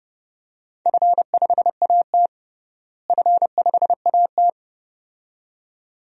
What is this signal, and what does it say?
Morse code, heard as a keyed tone, about 30 wpm, 700 Hz. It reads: F5AT F5AT
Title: DXCC-100 Call Signs - 4 Characters - Repeated 2x - 30wpm